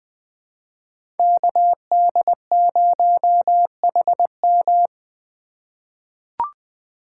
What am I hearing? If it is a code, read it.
Morse code, heard as a keyed tone, about 20 wpm, 700 Hz. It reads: KD0HM
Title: USA Call Signs - 5 Characters - 4x Character Spacing - 20wpm